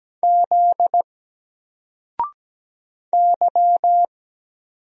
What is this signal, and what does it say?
Morse code, heard as a keyed tone, about 17 wpm, 700 Hz. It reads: Z Y
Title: Single Letter-Number 17wpm